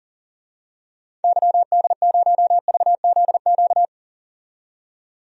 Morse code, heard as a keyed tone, about 40 wpm, 700 Hz. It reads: YD0VZQ
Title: DXCC-100 Call Signs - 6 Characters - 40wpm